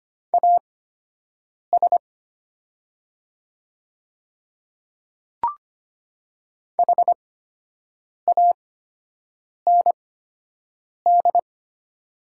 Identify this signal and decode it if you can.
Morse code, heard as a keyed tone, about 25 wpm, spaced as for 7 wpm, 700 Hz. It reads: AS HAND
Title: 8x Character Spacing - Top 200 Words - 25wpm